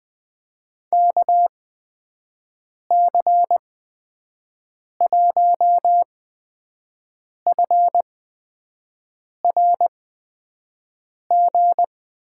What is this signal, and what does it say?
Morse code, heard as a keyed tone, about 20 wpm, spaced as for 5 wpm, 700 Hz. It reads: KC1FRG
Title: USA Call Signs - 6 Characters - 8x Character Spacing - 20wpm